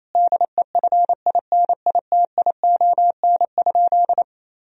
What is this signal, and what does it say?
Morse code, heard as a keyed tone, about 28 wpm, 700 Hz. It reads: DEFINITION?
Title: Sentences from Top 1000 Words - Review Part 4 - 28wpm